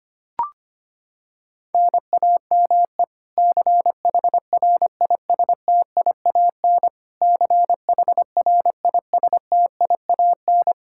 Morse code, heard as a keyed tone, about 25 wpm, 700 Hz. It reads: NAME CHRISTIAN CHRISTIAN
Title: Practice QSO - Exchange 1 - Beginning - 25wpm